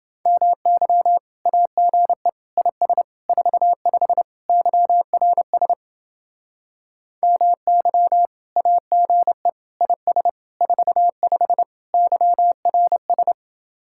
Morse code, heard as a keyed tone, about 31 wpm, 700 Hz. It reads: MY AGE IS 45 YRS MY AGE IS 45 YRS